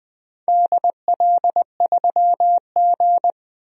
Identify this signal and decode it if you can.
Morse code, heard as a keyed tone, about 21 wpm, 700 Hz. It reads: DL3G
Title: DXCC-100 Call Signs - 4 Characters - 20wpm